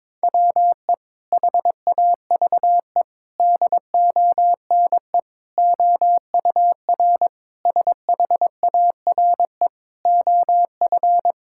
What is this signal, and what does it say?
Morse code, heard as a keyed tone, about 22 wpm, 700 Hz. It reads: WE HAVE DONE OUR SHARE OF
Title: Sentences from Top 1000 Words - Review Part 6 - 22wpm